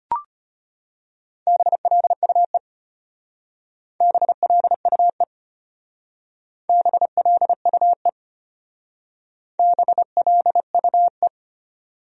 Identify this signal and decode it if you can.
Morse code, heard as a keyed tone, about 31 wpm, 700 Hz. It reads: BLUE BLUE BLUE BLUE
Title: Speed-Racing Top 500 Words 25wpm